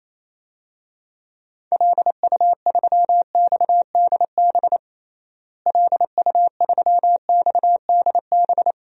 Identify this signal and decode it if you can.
Morse code, heard as a keyed tone, about 28 wpm, 700 Hz. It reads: LU3XDB LU3XDB
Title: DXCC-100 Call Signs - 6 Characters - Repeated 3x - 28wpm